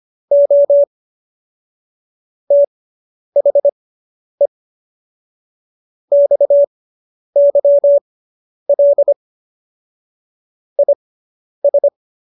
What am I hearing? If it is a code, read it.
Morse code, heard as a keyed tone, about 25 wpm, spaced as for 10 wpm, 570 Hz. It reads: O THE XYL IS